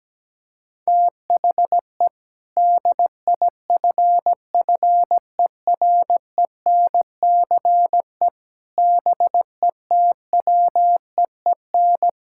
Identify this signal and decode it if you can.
Morse code, heard as a keyed tone, about 17 wpm, 700 Hz. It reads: THE DIFFERENCE BETWEEN
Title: Sentences from Top 1600 Words 17wpm